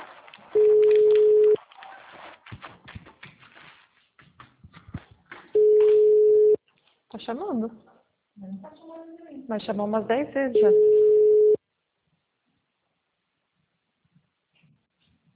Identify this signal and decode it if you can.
Ringback tone (Europe, 425 Hz), with speech.
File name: out- -2029- - - .2975